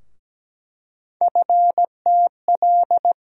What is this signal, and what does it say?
Morse code, heard as a keyed tone, about 17 wpm, 700 Hz. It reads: FTL
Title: USA Call Sign Postfixes - 3 Letters 17wpm